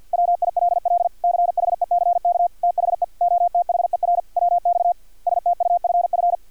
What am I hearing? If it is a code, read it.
Morse code, heard as a keyed tone, about 50 wpm, 700 Hz. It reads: WILL CHECK THE OTHER WX STUFF
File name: archived / CRSnet